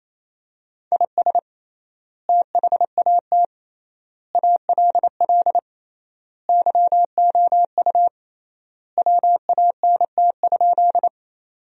Morse code, heard as a keyed tone, about 27 wpm, 700 Hz. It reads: IS THAT ALL YOU WANT?
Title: 3x Word-Spacing - Sentences from Top 200 Words - 28wpm